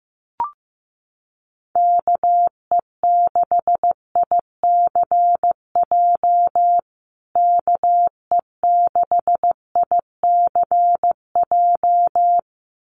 Morse code, heard as a keyed tone, about 15 wpm, 700 Hz. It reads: KE6ICJ KE6ICJ
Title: USA Call Signs Repeated 2x 15wpm